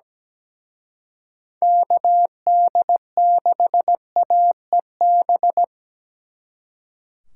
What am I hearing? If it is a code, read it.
Morse code, heard as a keyed tone, about 20 wpm, 700 Hz. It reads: KD6AEB
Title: Speed Racing - USA 2x3 Call Signs 17wpm